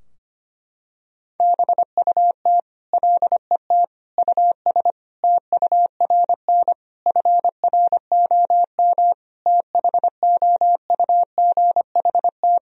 Morse code, heard as a keyed tone, about 25 wpm, 700 Hz. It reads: BUT LET US TURN FROM THOUGHT
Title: Sentences from Top 1000 Words - Review Part 4 - 25wpm